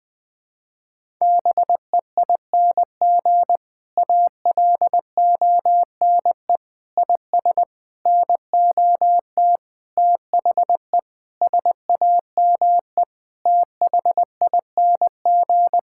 Morse code, heard as a keyed tone, about 20 wpm, 700 Hz. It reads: BEING ALONE IS NOT THE SAME THING
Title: Sentences with 6-Word Phrases - Spoken Beginning - 20wpm